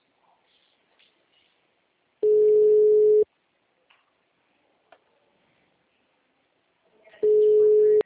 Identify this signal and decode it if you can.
Ringback tone (Europe, 425 Hz).
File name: out- -2003- - - .891